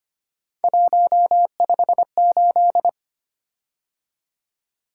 Morse code, heard as a keyed tone, about 25 wpm, 700 Hz. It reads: 158
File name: Three Letter-Number Combinations - 25wpm